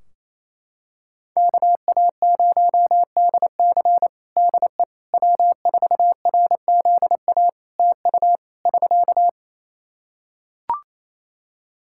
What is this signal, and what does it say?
Morse code, heard as a keyed tone, about 28 wpm, 700 Hz. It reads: KA0DC DE W4RZA TU <SK>
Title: Practice QSO - Exchange 4 - Ending - 28wpm